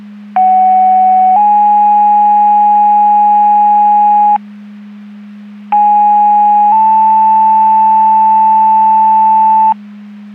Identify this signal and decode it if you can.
Two-tone page, heard as a one-second tone, then a three-second tone, about 750 then 830 Hz, about 830 then 880 Hz. It is heard from 0.4 to 9.7 s.
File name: Portage 154.385 Paging